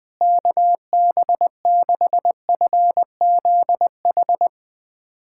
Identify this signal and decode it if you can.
Morse code, heard as a keyed tone, about 21 wpm, 700 Hz. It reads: KB6FZH
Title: USA Callsign DE Callsign 20wpm